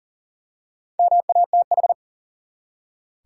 Morse code, heard as a keyed tone, about 40 wpm, 700 Hz. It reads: MATH
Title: Four Letter Words - 40wpm